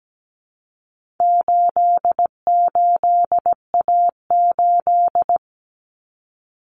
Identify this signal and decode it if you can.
Morse code, heard as a keyed tone, about 17 wpm, 700 Hz. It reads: 88A8
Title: Common Cut Numbers - 4-Digit 17wpm